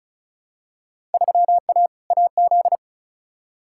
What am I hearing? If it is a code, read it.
Morse code, heard as a keyed tone, about 36 wpm, 700 Hz. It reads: 3A AZ
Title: Field Day Exchanges - 35wpm